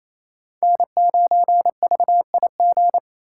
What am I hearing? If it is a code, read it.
Morse code, heard as a keyed tone, about 28 wpm, 700 Hz. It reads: N9VIG
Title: USA Call Signs - 4x Character Spacing - 28wpm